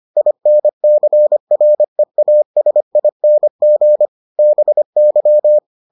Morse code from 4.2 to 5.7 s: BY

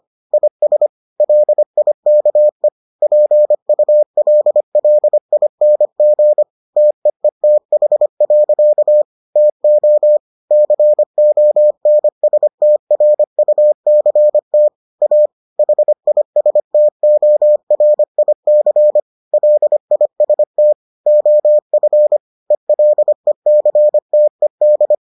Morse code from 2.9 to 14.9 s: PULLING TEETH. TO CONSTRUCT

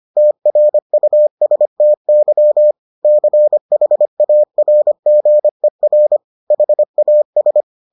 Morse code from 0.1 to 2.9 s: TRUSTY

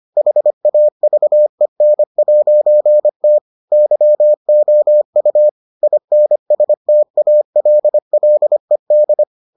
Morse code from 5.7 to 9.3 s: INSTALLED

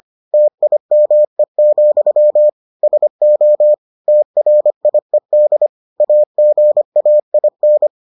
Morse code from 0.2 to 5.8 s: TIME, SO TRIED

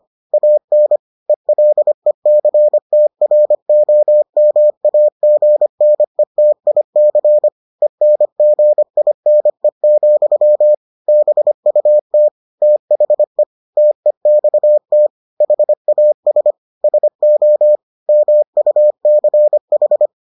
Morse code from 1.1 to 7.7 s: ELECTROMAGNETIC

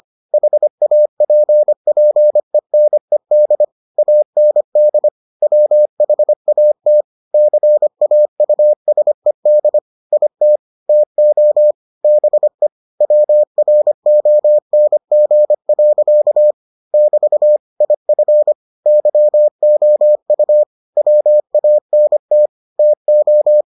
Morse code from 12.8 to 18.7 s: WRONG. <BT> IF

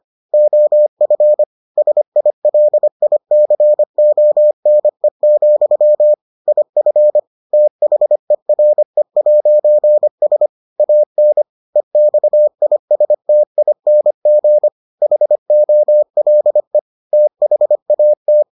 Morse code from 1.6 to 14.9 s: SILICONE, IF THERE'S AN EXISTING

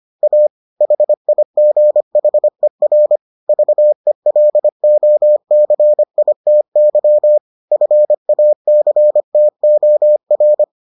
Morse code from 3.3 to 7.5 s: VELOCITY

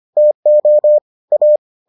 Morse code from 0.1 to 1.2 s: TO

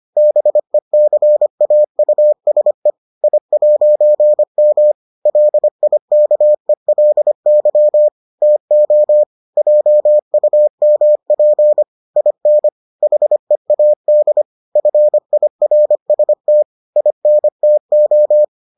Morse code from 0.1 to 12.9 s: BECAUSE I'M LIKELY TO JUMP IN

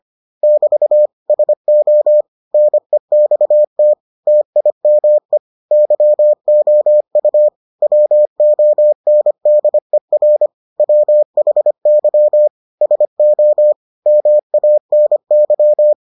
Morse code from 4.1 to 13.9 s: TIME YOU WONDER WHY SO

